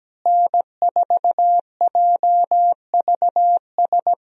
Morse code, heard as a keyed tone, about 17 wpm, 700 Hz. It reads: N4JVS
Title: USA Call Signs - 5 Characters - Repeated 3x - 17wpm